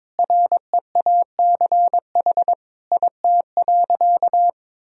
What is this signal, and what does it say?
Morse code, heard as a keyed tone, about 21 wpm, 700 Hz. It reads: REACH IT.
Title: Sentences from Top 400 Words and Abbreviations 22wpm